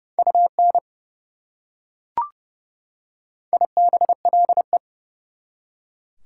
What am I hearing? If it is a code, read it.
Morse code, heard as a keyed tone, about 30 wpm, 700 Hz. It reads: UN IBLE
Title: Prefixes and Suffixes 30wpm